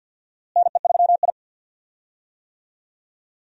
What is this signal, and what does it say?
Morse code, heard as a keyed tone, about 52 wpm, 700 Hz. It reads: NE3I
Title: USA Call Signs - 4 Characters - Repeated 3x - 50wpm